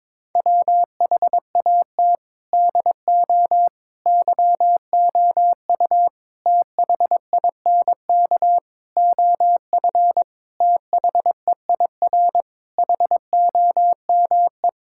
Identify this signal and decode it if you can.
Morse code, heard as a keyed tone, about 22 wpm, 700 Hz. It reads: WHAT DO YOU THINK OF THEIR HOME
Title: Sentences from Top 1000 Words - Review Part 5 - 22wpm